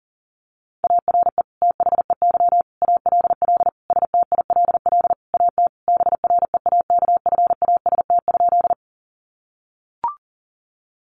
Morse code, heard as a keyed tone, about 40 wpm, 700 Hz. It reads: ARE THEY ALL STILL AT BREAKFAST?
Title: Sentences from Top 1500 Words 40wpm